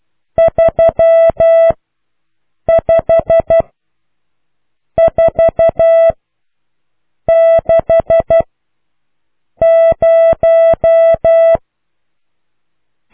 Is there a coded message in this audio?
Yes: Morse code from 0.4 to 11.6 s, heard as a keyed tone, about 12 wpm, spaced as for 6 wpm, 660 Hz. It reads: 35460